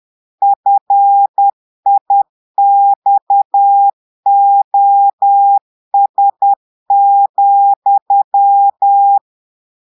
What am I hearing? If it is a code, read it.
Morse code, heard as a keyed tone, about 10 wpm, 800 Hz. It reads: FIXOS,
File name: portugais texte en clair 2 vitesse 10 mots minutes